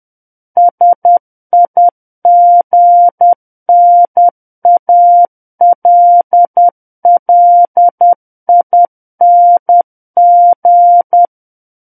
Morse code, heard as a keyed tone, about 10 wpm, 700 Hz. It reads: SIGNALLING
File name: Book One / 01-Chapter 1 - 10 WPM